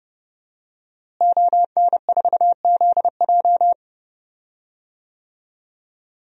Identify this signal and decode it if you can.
Morse code, heard as a keyed tone, about 30 wpm, 700 Hz. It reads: ON4ZJ
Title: DXCC-100 Call Signs - 5 Characters - Repeated 2x - 30wpm